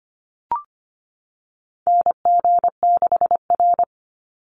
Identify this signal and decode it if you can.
Morse code, heard as a keyed tone, about 25 wpm, 700 Hz. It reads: NG6R
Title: NA SOTA Chaser Callsigns 25wpm